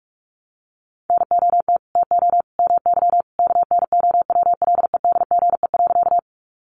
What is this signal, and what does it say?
Morse code, heard as a keyed tone, about 44 wpm, 700 Hz. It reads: NOT TO MY KNOWLEDGE.
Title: Sentences from Top 600 Words 45wpm